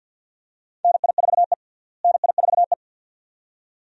Morse code, heard as a keyed tone, about 47 wpm, 700 Hz. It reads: NI4E NI4E
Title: USA Call Signs - 4 Characters - Repeated 2x - 50wpm